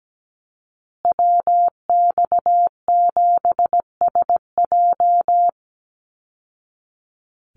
Morse code, heard as a keyed tone, about 17 wpm, 700 Hz. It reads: WX7SJ
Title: Call Signs Repeated 2x - 17wpm